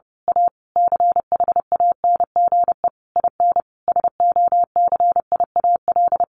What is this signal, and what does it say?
Morse code, heard as a keyed tone, about 30 wpm, 700 Hz. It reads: A CHANGE IN SOCIAL
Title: A Sound Of Thunder 30wpm